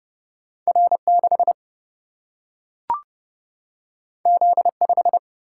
Morse code, heard as a keyed tone, about 30 wpm, 700 Hz. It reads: R6 Z5
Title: Single-Letter Single-Number 30wpm